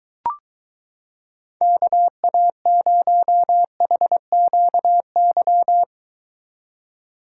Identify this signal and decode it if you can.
Morse code, heard as a keyed tone, about 23 wpm, 700 Hz. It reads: KA0HQY